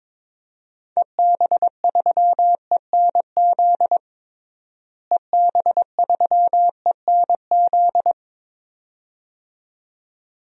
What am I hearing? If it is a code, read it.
Morse code, heard as a keyed tone, about 22 wpm, 700 Hz. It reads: EB3ENZ EB3ENZ
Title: DXCC-100 Call Signs - 6 Characters - Repeated 2x - 22wpm